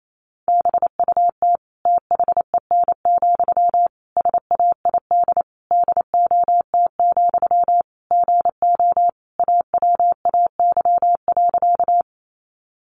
Morse code, heard as a keyed tone, about 28 wpm, 700 Hz. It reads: BUT THEN, SAID DOT, GO AWAY.
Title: Sentences from Top 1500 Words 28wpm